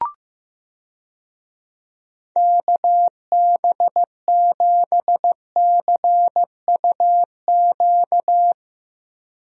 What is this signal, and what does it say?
Morse code, heard as a keyed tone, about 15 wpm, 700 Hz. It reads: KB7CUQ
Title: USA Call Signs - 6 Characters - Repeated 3x - 15wpm